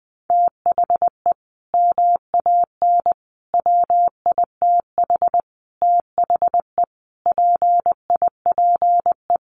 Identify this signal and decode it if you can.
Morse code, heard as a keyed tone, about 20 wpm, 700 Hz. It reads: THE MAN WITH THE PIPE
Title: Sentences from Top 1500 Words 20wpm